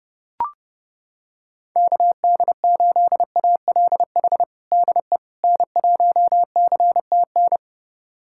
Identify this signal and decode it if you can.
Morse code, heard as a keyed tone, about 30 wpm, 700 Hz. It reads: KD8ALH DE N1CTN
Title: USA Callsign DE Callsign 30wpm